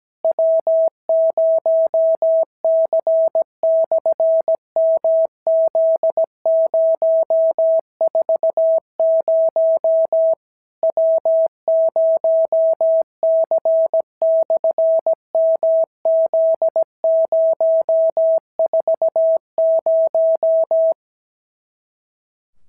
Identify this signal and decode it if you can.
Morse code, heard as a keyed tone, about 17 wpm, 650 Hz. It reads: W0C/MZ040 W0C/MZ040
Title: SOTA QSO Practice 17wpm